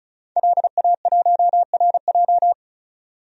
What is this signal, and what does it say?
Morse code, heard as a keyed tone, about 34 wpm, 700 Hz. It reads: LA1RJ
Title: DXCC-100 Call Signs - 5 Characters - 35wpm